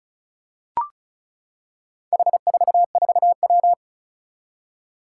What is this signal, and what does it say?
Morse code, heard as a keyed tone, about 36 wpm, 700 Hz. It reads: H44W
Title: Four Challenging Letter-Number Combinations 35wpm